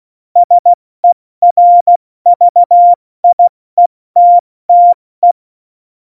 Morse code, heard as a keyed tone, about 16 wpm, spaced as for 14 wpm, 700 Hz. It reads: SERVIETTE
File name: Morsekurs / Witze / Witz 09. April 2019, 70 F80